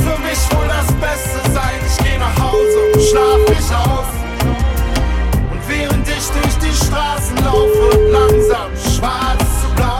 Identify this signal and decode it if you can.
Ringback tone (Europe, 425 Hz), with noise.